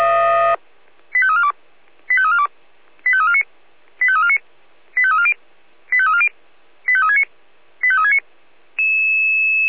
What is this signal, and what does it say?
Five-tone selective call (ZVEI), heard as quick runs of five tones, tones of about 70 ms. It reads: 85342 85342 85349 85349 85340 85340 85369 85369